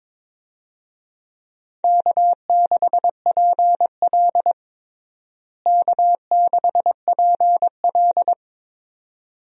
Morse code, heard as a keyed tone, about 22 wpm, 700 Hz. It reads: K6PL K6PL
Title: USA Call Signs - 4 Characters - Repeated 2x - 22wpm